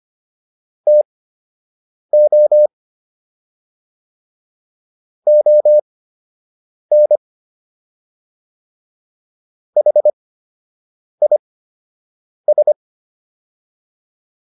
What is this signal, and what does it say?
Morse code, heard as a keyed tone, about 25 wpm, spaced as for 7 wpm, 600 Hz. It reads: TO ON HIS